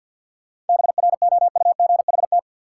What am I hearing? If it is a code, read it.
Morse code, heard as a keyed tone, about 53 wpm, 700 Hz. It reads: BROUGHT
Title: Sentences from Top 500 Words - New Word Drill - 50wpm